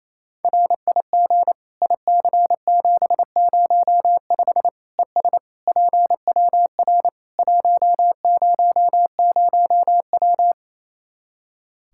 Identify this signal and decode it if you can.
Morse code, heard as a keyed tone, about 28 wpm, 700 Hz. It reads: RIG IC705 ES PWR 100W
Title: Practice QSO - Exchange 2 - Info - 28wpm